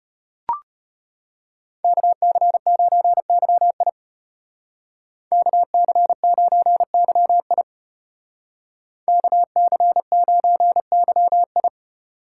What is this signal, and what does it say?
Morse code, heard as a keyed tone, about 33 wpm, 700 Hz. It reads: KC9YI KC9YI KC9YI